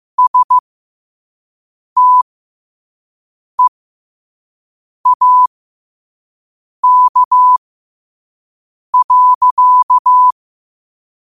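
Morse code, heard as a keyed tone, about 15 wpm, spaced as for 5 wpm, 1000 Hz. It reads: STEAK.
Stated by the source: Morse quote of the day at 5 wpm.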